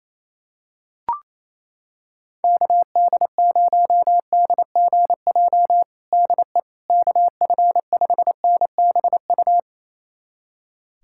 Morse code, heard as a keyed tone, about 28 wpm, 700 Hz. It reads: KD0DGJ DE KF5NBU
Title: USA Callsign DE Callsign - 6 Characters - 28wpm